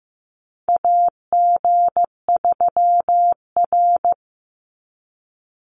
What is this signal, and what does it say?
Morse code, heard as a keyed tone, about 15 wpm, 700 Hz. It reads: AG3R
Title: Call Signs Repeated 3x - 15wpm